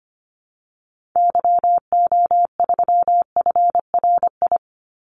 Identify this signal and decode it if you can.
Morse code, heard as a keyed tone, about 25 wpm, 700 Hz. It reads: YO3FRI
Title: Call Signs Repeated 3x - 25wpm